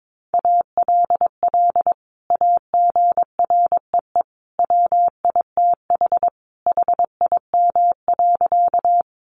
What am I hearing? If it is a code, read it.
Morse code, heard as a keyed tone, about 22 wpm, 700 Hz. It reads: ALL AGREE WITH HIM.
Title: Sentences from Top 1100 Words 22wpm